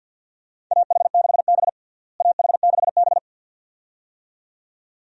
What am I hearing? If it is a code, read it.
Morse code, heard as a keyed tone, about 51 wpm, 700 Hz. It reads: AH6B AH6B
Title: USA Call Signs - 4 Characters - Repeated 2x - 50wpm